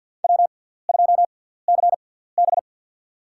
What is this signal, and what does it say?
Morse code, heard as a keyed tone, about 53 wpm, spaced as for 17 wpm, 700 Hz. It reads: W2CB
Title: USA Call Signs - 4 Characters - 6x Character Spacing - 50wpm